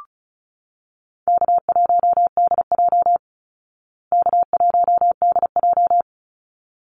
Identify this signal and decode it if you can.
Morse code, heard as a keyed tone, about 34 wpm, 700 Hz. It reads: K1DJ K1DJ
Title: Easy Call Signs Repeated 2x - 35wpm